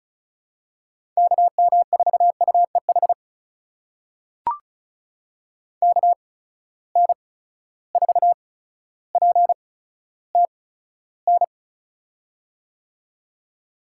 Morse code, heard as a keyed tone, about 34 wpm, spaced as for 15 wpm, 700 Hz. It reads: KM4UEH KN4PTN
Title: USA Call Signs - 6 Characters - 8x Character Spacing - 35wpm